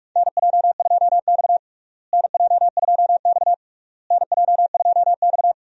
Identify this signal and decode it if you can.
Morse code, heard as a keyed tone, about 42 wpm, 700 Hz. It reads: NJ2X NJ2X NJ2X
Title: USA Call Signs - 4 Characters - Repeated 3x - 45wpm